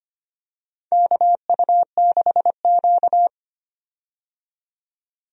Morse code, heard as a keyed tone, about 25 wpm, 700 Hz. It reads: KU6Q
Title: USA Call Signs - 4 Characters - 25wpm